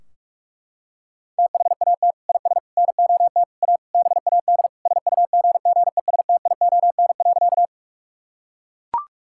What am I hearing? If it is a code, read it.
Morse code, heard as a keyed tone, about 42 wpm, 700 Hz. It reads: THAT IS NOT A BAD SUGGESTION.
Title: Sentences from Top 1600 Words 45wpm